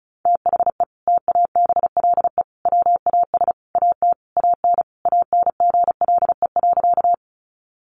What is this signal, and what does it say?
Morse code, heard as a keyed tone, about 34 wpm, 700 Hz. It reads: THE TABLE WAS AT AN ANGLE.
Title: Sentences from Top 1500 Words 35wpm